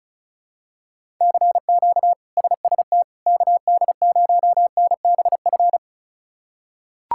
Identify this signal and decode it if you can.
Morse code, heard as a keyed tone, about 34 wpm, 700 Hz. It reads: CQ SST KD0NBF
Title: K1USN SST QSO Practice - 35wpm